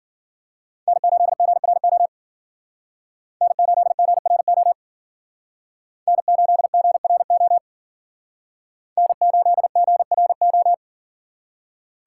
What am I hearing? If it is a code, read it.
Morse code, heard as a keyed tone, about 54 wpm, 700 Hz. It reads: N8GRO N8GRO N8GRO N8GRO